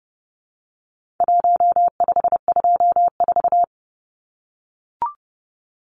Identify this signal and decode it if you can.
Morse code, heard as a keyed tone, about 30 wpm, 700 Hz. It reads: 1524
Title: Numbers - 4-Digit 30wpm